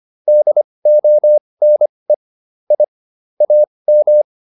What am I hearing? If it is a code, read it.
Morse code, heard as a keyed tone, about 25 wpm, spaced as for 20 wpm, 600 Hz. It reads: DONE I AM